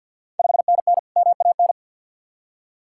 Morse code, heard as a keyed tone, about 46 wpm, 690 Hz. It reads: 5NN MAN
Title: Ontario QSO Party Exchange 50wpm